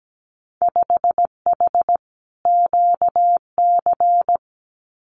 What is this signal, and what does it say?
Morse code, heard as a keyed tone, about 17 wpm, 700 Hz. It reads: 5H QC
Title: Winter Field Day Exchanges Encore 17wpm